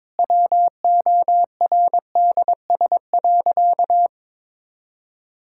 Morse code, heard as a keyed tone, about 22 wpm, 700 Hz. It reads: WORDS.
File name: Difficult Sentences with Hints - 22wpm